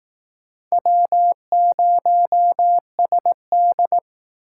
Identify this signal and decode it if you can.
Morse code, heard as a keyed tone, about 20 wpm, 700 Hz. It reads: W0SD